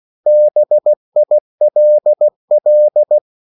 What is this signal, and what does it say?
Morse code, heard as a keyed tone, about 16 wpm, 600 Hz. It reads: BILL
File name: CW / Practice / 500MostCommonEnglishWords16